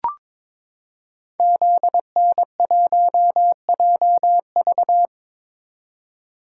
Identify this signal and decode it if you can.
Morse code, heard as a keyed tone, about 22 wpm, 700 Hz. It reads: ZN1JV
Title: Code Groups - 5 Letter:Numbers - 22wpm